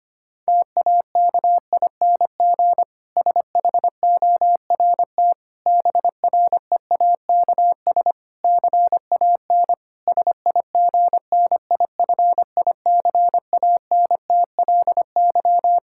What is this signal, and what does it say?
Morse code, heard as a keyed tone, about 25 wpm, 700 Hz. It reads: TAKING SHORT BREAKS CAN SIGNIFICANTLY
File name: practice / 25wpm / Difficult Sentences with Hints - 25wpm